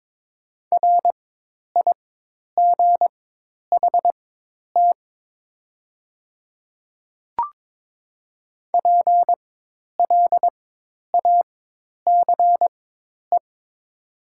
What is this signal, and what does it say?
Morse code, heard as a keyed tone, about 22 wpm, spaced as for 10 wpm, 700 Hz. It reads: RIGHT PLACE
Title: 4x Character Spacing - Top 200 Words - 22wpm